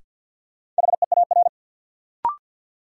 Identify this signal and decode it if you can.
Morse code, heard as a keyed tone, about 50 wpm, 700 Hz. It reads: HEAR
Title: Sentences from Top 300 Words - New Word Drill - 50wpm